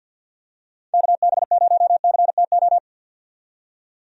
Morse code, heard as a keyed tone, about 46 wpm, 700 Hz. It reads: KB0CTO
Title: USA Call Signs - 6 Characters - 50wpm